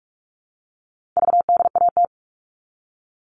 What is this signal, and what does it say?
Morse code, heard as a keyed tone, about 44 wpm, 700 Hz. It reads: VDAT
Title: All Cut Numbers - 4-Digit 45wpm